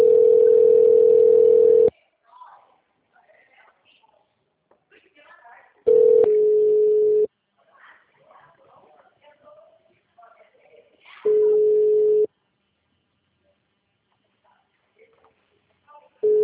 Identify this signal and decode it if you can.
Ringback tone (North America), 440 + 480 Hz, with speech.
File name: out- -2023- - - .976